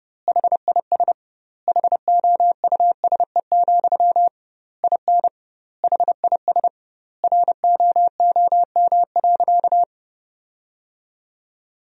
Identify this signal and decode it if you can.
Morse code, heard as a keyed tone, about 30 wpm, 700 Hz. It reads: HIS HOUSE, IN HIS ROOM.
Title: 2x Word-Spacing - Sentences from Top 300 Words - 30wpm